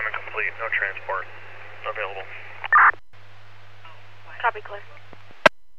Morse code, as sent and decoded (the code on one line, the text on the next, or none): none